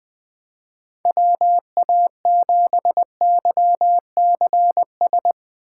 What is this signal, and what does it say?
Morse code, heard as a keyed tone, about 21 wpm, 700 Hz. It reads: WA7YCS
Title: Speed Racing - USA 2x3 Call Signs 20wpm